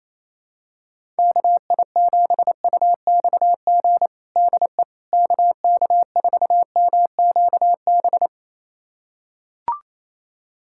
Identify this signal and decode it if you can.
Morse code, heard as a keyed tone, about 28 wpm, 700 Hz. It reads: KI7UXG DE KK4MQB
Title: USA Callsign DE Callsign - 6 Characters - 28wpm